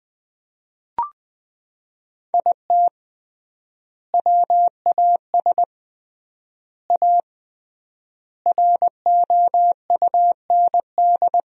Morse code, heard as a keyed tone, about 20 wpm, 700 Hz. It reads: IT WAS A ROUND
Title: 3x Word-Spacing - Sentences from Top 400 Words - 20wpm